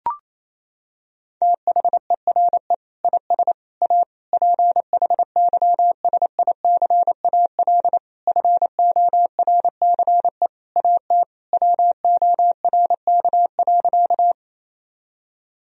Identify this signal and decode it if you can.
Morse code, heard as a keyed tone, about 28 wpm, 700 Hz. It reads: THERE IS A PHYSICAL FORCE AT WORK.
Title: Sentences from Top 1000 Words - Review Part 4 - 28wpm